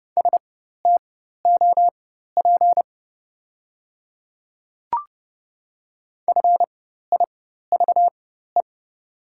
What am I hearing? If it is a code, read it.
Morse code, heard as a keyed tone, about 30 wpm, spaced as for 14 wpm, 690 Hz. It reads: STOP FIVE
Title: 4x Character Spacing - Top 500 Words - 30wpm